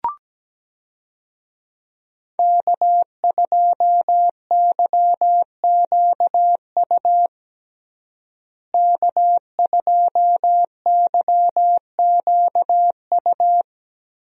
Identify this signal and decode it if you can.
Morse code, heard as a keyed tone, about 17 wpm, 700 Hz. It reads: K2YQU K2YQU
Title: USA Call Signs - 5 Characters - Repeated 3x - 17wpm